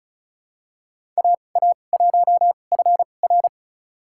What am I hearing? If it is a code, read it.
Morse code, heard as a keyed tone, about 36 wpm, spaced as for 25 wpm, 700 Hz. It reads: AA1FR
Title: USA Call Signs - 5 Characters - 2x Character Spacing - 35wpm